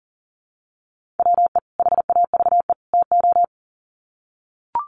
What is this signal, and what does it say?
Morse code, heard as a keyed tone, about 40 wpm, 700 Hz. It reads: WE HAVE TO
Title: Sets of 3 Words 40wpm